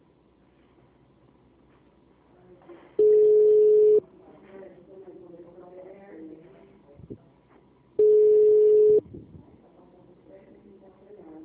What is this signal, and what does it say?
Ringback tone (Europe, 425 Hz), with speech.
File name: out- -2006- - - .351